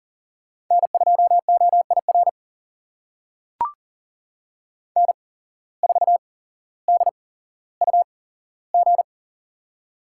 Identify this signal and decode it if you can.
Morse code, heard as a keyed tone, about 40 wpm, spaced as for 17 wpm, 700 Hz. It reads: N2OIR N4DUG
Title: USA Call Signs - 5 Characters - 8x Character Spacing - 40wpm